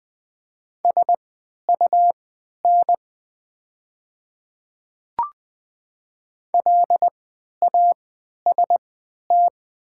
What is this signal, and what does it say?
Morse code, heard as a keyed tone, about 20 wpm, spaced as for 11 wpm, 700 Hz. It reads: SUN LAST
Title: 3x Character Spacing - Top 300 Words - 20wpm